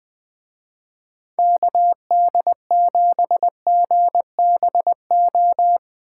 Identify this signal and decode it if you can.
Morse code, heard as a keyed tone, about 21 wpm, 700 Hz. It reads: KD7GBO
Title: Speed Racing - USA 2x3 Call Signs 20wpm